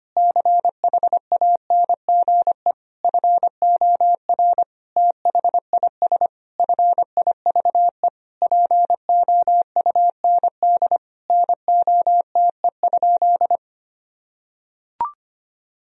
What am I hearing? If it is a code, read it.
Morse code, heard as a keyed tone, about 25 wpm, 700 Hz. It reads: CHANGE FOR THIS FIVE POUND NOTE?
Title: Sentences from Top 1000 Words - Review Part 2 - 25wpm